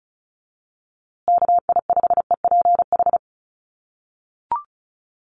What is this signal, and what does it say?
Morse code, heard as a keyed tone, about 34 wpm, 700 Hz. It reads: KI5EPH
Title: USA Call Signs Repeated 2x 35wpm